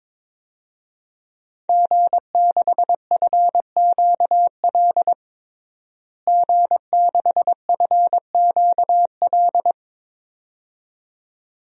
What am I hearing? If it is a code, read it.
Morse code, heard as a keyed tone, about 22 wpm, 700 Hz. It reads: G6FQL G6FQL
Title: DXCC-100 Call Signs - 5 Characters - Repeated 2x - 22wpm